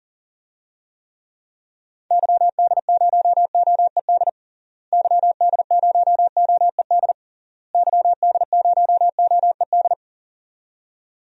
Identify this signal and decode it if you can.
Morse code, heard as a keyed tone, about 40 wpm, 700 Hz. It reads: YD0OED YD0OED YD0OED
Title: DXCC-100 Call Signs - 6 Characters - Repeated 3x - 40wpm